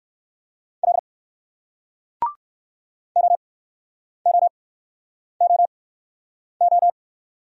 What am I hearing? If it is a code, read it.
Morse code, heard as a keyed tone, about 58 wpm, spaced as for 9 wpm, 700 Hz. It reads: L OOOO